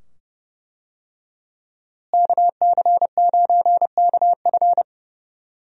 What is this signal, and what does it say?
Morse code, heard as a keyed tone, about 31 wpm, 700 Hz. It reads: KC9KF
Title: USA Call Signs - 5 Characters - Repeated 2x - 30wpm